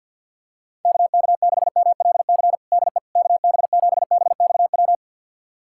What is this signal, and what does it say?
Morse code, heard as a keyed tone, about 52 wpm, 700 Hz. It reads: KK6MLC DE KB7BXW
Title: USA Callsign DE Callsign 50wpm